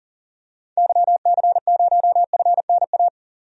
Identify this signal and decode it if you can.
Morse code, heard as a keyed tone, about 40 wpm, 700 Hz. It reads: YC0FNA
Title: DXCC-100 Call Signs - 6 Characters - 40wpm